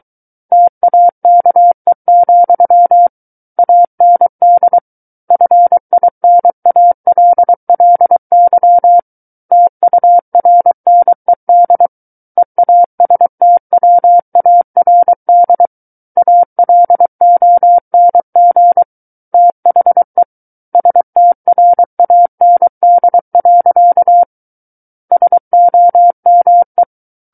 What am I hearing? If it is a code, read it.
Morse code, heard as a keyed tone, about 23 wpm, 700 Hz. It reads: TAKE, AND FINALLY TURNED EASTWARD ALONG THE STRAND. SOME
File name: Book One / 14-Chapter 14 - 23 WPM